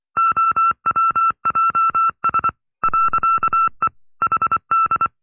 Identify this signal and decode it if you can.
Morse code, heard as a keyed tone, about 24 wpm, 1340 Hz. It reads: OWJS .E HD
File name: FreeFiles / CW-practice / 20WPM-side2